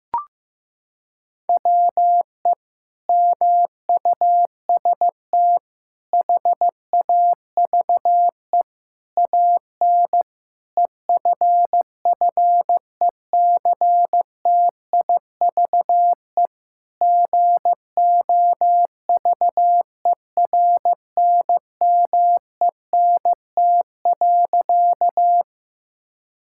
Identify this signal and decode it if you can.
Morse code, heard as a keyed tone, about 16 wpm, 700 Hz. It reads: WE MUST HAVE AN EFFECTIVE GOVERNMENT.
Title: Sentences from Top 1000 Words - Review Part 6 - 15wpm